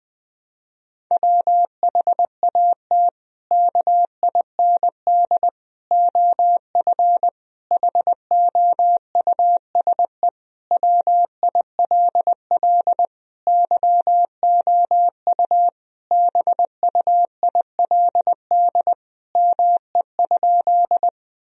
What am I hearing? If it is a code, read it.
Morse code, heard as a keyed tone, about 21 wpm, 700 Hz. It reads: WHAT KIND OF HOUSE WILL YOU BUILD ME?
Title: Sentences from Top 1000 Words - Review Part 1 - 20wpm